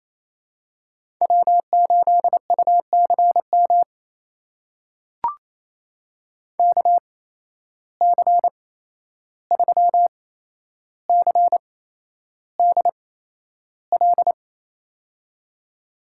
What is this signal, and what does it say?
Morse code, heard as a keyed tone, about 28 wpm, spaced as for 11 wpm, 700 Hz. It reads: W8UCM KC3CDL
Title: USA Call Signs - 8x Character Spacing - 28wpm